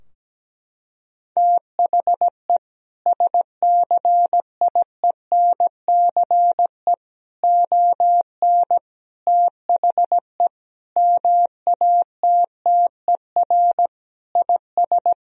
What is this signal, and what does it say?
Morse code, heard as a keyed tone, about 17 wpm, 700 Hz. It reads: THE SCIENCE ON THE MATTER IS
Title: Sentences from Top 1000 Words - Review Part 2 - 17wpm